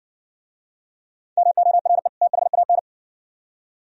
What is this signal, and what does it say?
Morse code, heard as a keyed tone, about 57 wpm, 700 Hz. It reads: MORE THAN